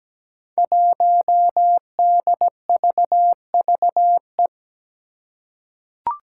Morse code, heard as a keyed tone, about 17 wpm, 700 Hz. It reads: 1DVVE
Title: Five Challenging Letter-Number Combinations 17wpm